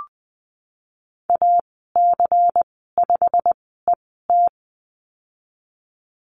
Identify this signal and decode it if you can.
Morse code, heard as a keyed tone, about 19 wpm, spaced as for 14 wpm, 700 Hz. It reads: AC5ET
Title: 238 Lesson 38g - Letter q - Call Signs 20wpm